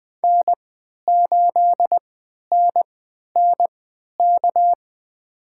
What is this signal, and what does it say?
Morse code, heard as a keyed tone, about 21 wpm, spaced as for 11 wpm, 700 Hz. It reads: N8NNK
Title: USA Call Signs - 3x Character Spacing - 20wpm